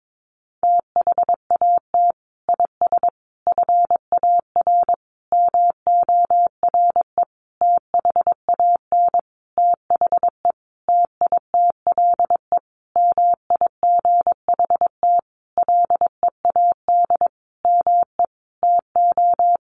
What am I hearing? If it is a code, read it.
Morse code, heard as a keyed tone, about 22 wpm, 700 Hz. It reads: THAT IS FAR MORE THAN THE TITLE MIGHT LEAD ME TO
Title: Sentences from Top 1200 Words 22wpm